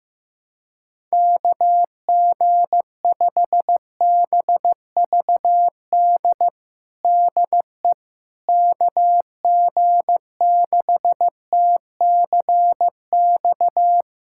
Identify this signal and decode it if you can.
Morse code, heard as a keyed tone, about 15 wpm, 700 Hz. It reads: KG5BVD DE KG6TCX
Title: USA Callsign DE Callsign 15wpm